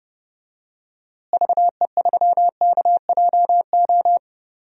Morse code, heard as a keyed tone, about 30 wpm, 700 Hz. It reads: VE3KJO
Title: DXCC-100 Call Signs - 6 Characters - Repeated 2x - 30wpm